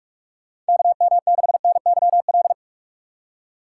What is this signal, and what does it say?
Morse code, heard as a keyed tone, about 41 wpm, 700 Hz. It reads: KM6NYL
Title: USA Call Signs - 6 Characters - 2x Character Spacing - 45wpm